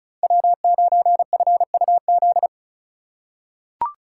Morse code, heard as a keyed tone, about 34 wpm, 700 Hz. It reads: W9FUZ
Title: USA Call Signs - 2x Character Spacing - 35wpm